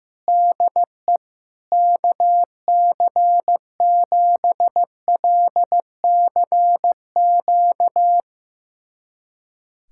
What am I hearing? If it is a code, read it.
Morse code, heard as a keyed tone, about 15 wpm, 700 Hz. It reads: DE KC7LCQ
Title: USA Callsign DE Callsign - 6 Characters - 15wpm